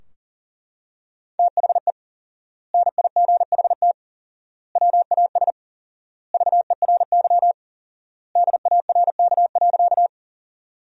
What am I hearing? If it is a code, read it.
Morse code, heard as a keyed tone, about 40 wpm, 700 Hz. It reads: THE NIGHT WAS VERY DARK.
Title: 4x Word-Spacing - Sentences from Top 500 Words - 40wpm